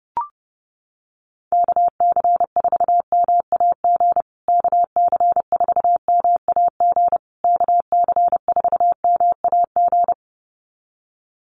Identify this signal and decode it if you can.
Morse code, heard as a keyed tone, about 28 wpm, 700 Hz. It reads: KC4MAG KC4MAG KC4MAG
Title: USA Call Signs Repeated 3x 30wpm